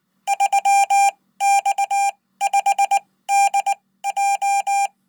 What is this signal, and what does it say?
Morse code, heard as a keyed tone, about 19 wpm, spaced as for 15 wpm, 800 Hz. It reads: 3X5DJ
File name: Code